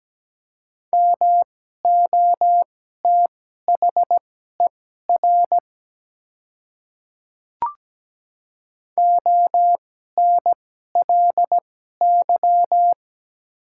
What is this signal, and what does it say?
Morse code, heard as a keyed tone, about 17 wpm, spaced as for 12 wpm, 700 Hz. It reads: MOTHER ONLY
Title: 2x Character Spacing - Top 200 Words - 17wpm